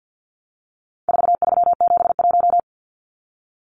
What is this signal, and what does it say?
Morse code, heard as a keyed tone, about 50 wpm, 700 Hz. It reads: 4371
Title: Numbers - 4-Digit 50wpm